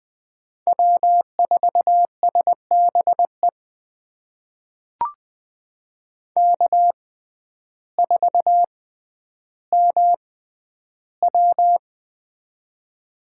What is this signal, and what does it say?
Morse code, heard as a keyed tone, about 20 wpm, spaced as for 11 wpm, 700 Hz. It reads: W4SBE K4MW
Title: USA Call Signs - 6x Character Spacing - 20wpm